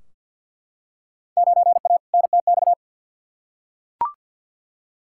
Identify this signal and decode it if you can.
Morse code, heard as a keyed tone, about 52 wpm, 700 Hz. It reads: 9A NTX
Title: Field Day Exchanges Encore - 50wpm